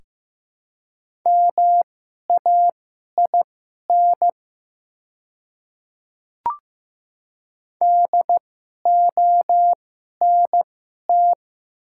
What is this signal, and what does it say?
Morse code, heard as a keyed tone, about 15 wpm, spaced as for 11 wpm, 700 Hz. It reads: MAIN DONT
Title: 2x Character Spacing - Top 500 Words - 15wpm